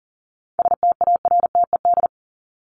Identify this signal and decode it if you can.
Morse code, heard as a keyed tone, about 40 wpm, 700 Hz. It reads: STARTED
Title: ED Suffix - Top 100 Words with Base Words 40wpm